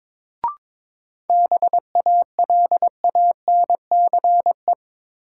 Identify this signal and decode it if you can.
Morse code, heard as a keyed tone, about 22 wpm, 700 Hz. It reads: BALANCE
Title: Sentences from Top 900 Words - New Word Drill - 22wpm